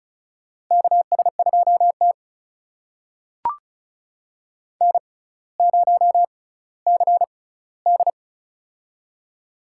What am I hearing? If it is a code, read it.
Morse code, heard as a keyed tone, about 36 wpm, spaced as for 18 wpm, 700 Hz. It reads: KS2T N0CD
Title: USA Call Signs - 4 Characters - 6x Character Spacing - 35wpm